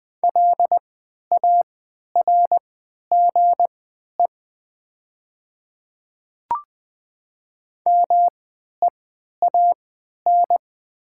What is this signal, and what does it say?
Morse code, heard as a keyed tone, about 20 wpm, spaced as for 11 wpm, 700 Hz. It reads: LARGE MEAN
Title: 3x Character Spacing - Top 500 Words - 20wpm